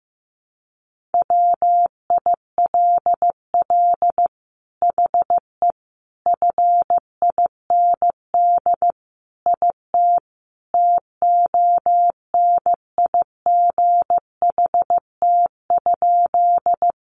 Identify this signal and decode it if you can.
Morse code, heard as a keyed tone, about 15 wpm, 700 Hz. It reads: WILL HE FIND IT TONIGHT?
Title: Sentences from Top 600 Words 15wpm